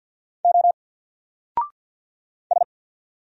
Morse code, heard as a keyed tone, about 50 wpm, 700 Hz. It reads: O S
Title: Single Letter-Number 50wpm